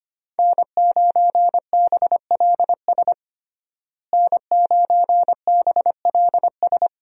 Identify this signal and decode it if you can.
Morse code, heard as a keyed tone, about 25 wpm, 700 Hz. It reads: N9BLS N9BLS
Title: USA Call Signs - 5 Characters - Repeated 2x - 25wpm